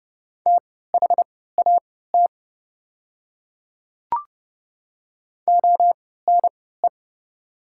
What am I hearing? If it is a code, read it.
Morse code, heard as a keyed tone, about 30 wpm, spaced as for 17 wpm, 690 Hz. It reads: THAT ONE
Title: 3x Character Spacing - Top 200 Words - 30wpm